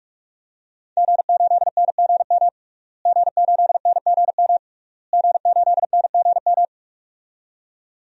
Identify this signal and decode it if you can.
Morse code, heard as a keyed tone, about 47 wpm, 700 Hz. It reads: G8NGM G8NGM G8NGM
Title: DXCC-100 Call Signs - 5 Characters - Repeated 3x - 45wpm